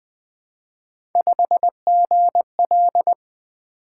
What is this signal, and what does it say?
Morse code, heard as a keyed tone, about 20 wpm, 700 Hz. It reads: 5GL
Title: Three Letter-Number Combinations - 20wpm